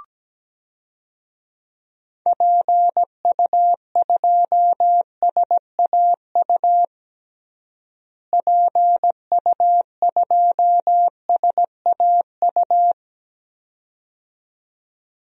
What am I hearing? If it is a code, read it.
Morse code, heard as a keyed tone, about 17 wpm, 700 Hz. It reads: PU2SAU PU2SAU
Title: DXCC-100 Call Signs - 6 Characters - Repeated 2x - 17wpm